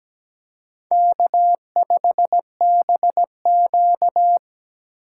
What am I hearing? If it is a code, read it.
Morse code, heard as a keyed tone, about 20 wpm, 700 Hz. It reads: K5BQ
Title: Speed Racing - USA Call Signs 17wpm